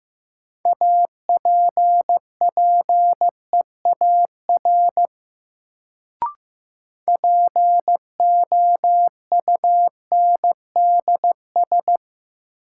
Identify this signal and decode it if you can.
Morse code, heard as a keyed tone, about 15 wpm, 700 Hz. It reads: APPEAR POUNDS
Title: Sentences from Top 1000 Words - Review Part 2 - New Word Drill - 15wpm